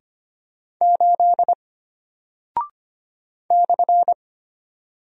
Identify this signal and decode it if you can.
Morse code, heard as a keyed tone, about 25 wpm, 700 Hz. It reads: 8 /
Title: Single Character (Letter:Number:Punctuation:Prosign) Alt 25wpm